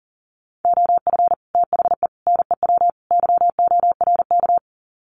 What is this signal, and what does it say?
Morse code, heard as a keyed tone, about 40 wpm, 700 Hz. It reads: OF THE NEW YORK
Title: Sets of 4 Words 40wpm